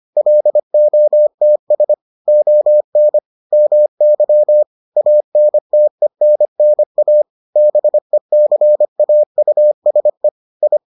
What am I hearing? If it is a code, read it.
Morse code, heard as a keyed tone, about 25 wpm, 600 Hz. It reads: LOTS ON MY ANTENNA BECAUSE I